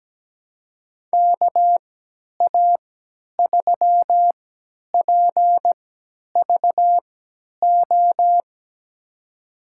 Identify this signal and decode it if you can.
Morse code, heard as a keyed tone, about 17 wpm, spaced as for 10 wpm, 700 Hz. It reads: KA3PVO
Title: USA Call Signs - 6 Characters - 3x Character Spacing - 17wpm